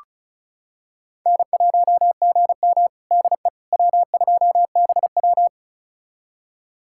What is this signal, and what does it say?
Morse code, heard as a keyed tone, about 36 wpm, 700 Hz. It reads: N1GM DE W2BW
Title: USA Callsign DE Callsign - 4 Characters - 35wpm